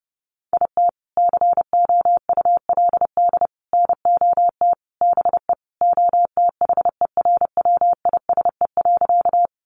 Morse code, heard as a keyed tone, about 30 wpm, 700 Hz. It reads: IT COULD NOT BE OTHERWISE.
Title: Sentences from Top 1200 Words 30wpm